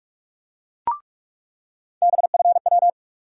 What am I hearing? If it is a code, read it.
Morse code, heard as a keyed tone, about 46 wpm, 700 Hz. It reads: BFW
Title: USA Call Sign Postfixes - 3 Letters 45wpm